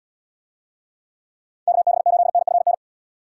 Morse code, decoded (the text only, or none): KB8NFN